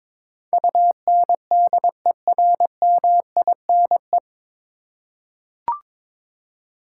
Morse code, heard as a keyed tone, about 21 wpm, 700 Hz. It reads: UNDERMINE
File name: 9 Letter Words - First 3 Letters Given - 22wpm